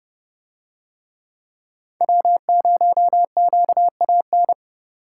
Morse code, heard as a keyed tone, about 27 wpm, 700 Hz. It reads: W0QAN